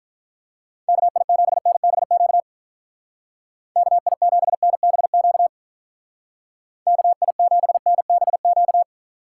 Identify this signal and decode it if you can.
Morse code, heard as a keyed tone, about 45 wpm, 700 Hz. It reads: KI7NBQ KI7NBQ KI7NBQ